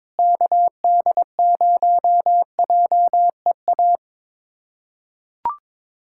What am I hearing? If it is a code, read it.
Morse code, heard as a keyed tone, about 21 wpm, 700 Hz. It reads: KD0JEA
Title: USA Callsign DE Callsign - 6 Characters - 22wpm